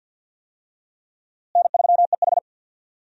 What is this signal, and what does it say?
Morse code, heard as a keyed tone, about 50 wpm, 700 Hz. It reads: N3EH
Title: USA Call Signs - 4 Characters - Repeated 3x - 50wpm